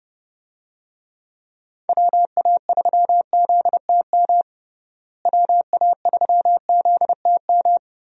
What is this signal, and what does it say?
Morse code, heard as a keyed tone, about 32 wpm, 700 Hz. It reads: WA3ZTM WA3ZTM
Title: USA Call Signs - 6 Characters - Repeated 2x - 30wpm